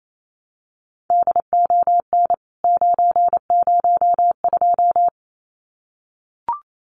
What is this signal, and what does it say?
Morse code, heard as a keyed tone, about 28 wpm, 700 Hz. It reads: DON 902
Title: CWT Exchange 28wpm